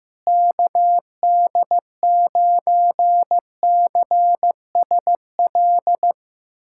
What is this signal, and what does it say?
Morse code, heard as a keyed tone, about 15 wpm, 700 Hz. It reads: KD9CSL
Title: USA Callsign DE Callsign - 6 Characters - 15wpm